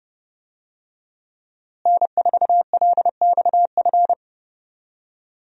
Morse code, heard as a keyed tone, about 29 wpm, 700 Hz. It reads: N4LXF
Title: USA Call Signs - 5 Characters - Repeated 2x - 30wpm